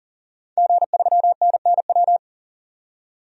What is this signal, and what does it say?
Morse code, heard as a keyed tone, about 38 wpm, 700 Hz. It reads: G3NNW
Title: DXCC-100 Call Signs - 5 Characters - 40wpm